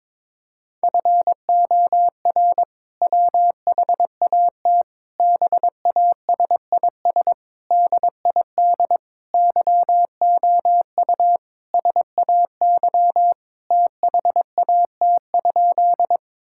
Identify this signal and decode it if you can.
Morse code, heard as a keyed tone, about 22 wpm, 700 Hz. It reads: FOR WHAT BASIS DID YOU SAY THAT?
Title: Sentences from Top 1000 Words - Review Part 5 - 22wpm